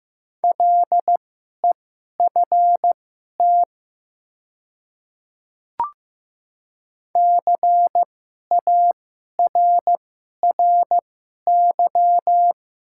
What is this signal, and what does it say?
Morse code, heard as a keyed tone, about 15 wpm, spaced as for 11 wpm, 700 Hz. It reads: LEFT CARRY
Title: 2x Character Spacing - Top 300 Words - 15wpm